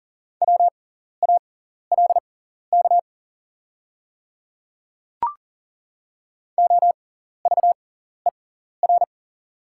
Morse code, heard as a keyed tone, about 37 wpm, spaced as for 14 wpm, 690 Hz. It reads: WALK OVER
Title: 6x Character Spacing - Top 500 Words - 40wpm